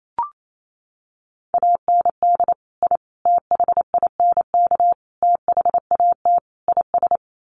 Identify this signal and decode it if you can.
Morse code, heard as a keyed tone, about 28 wpm, 700 Hz. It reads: AND I THINK THAT IS
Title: Sets of 5 Words Encore2 28wpm